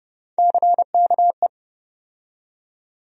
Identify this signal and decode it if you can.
Morse code, heard as a keyed tone, about 30 wpm, 700 Hz. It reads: CKE
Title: USA Call Sign Postfixes - 3 Letters 30wpm